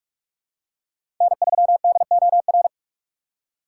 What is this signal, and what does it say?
Morse code, heard as a keyed tone, about 43 wpm, 700 Hz. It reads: N3DOR
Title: USA Call Signs - 5 Characters - 2x Character Spacing - 45wpm